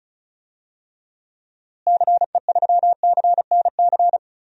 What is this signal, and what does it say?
Morse code, heard as a keyed tone, about 34 wpm, 700 Hz. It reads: CE3CNC
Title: DXCC-100 Call Signs - 6 Characters - Repeated 3x - 35wpm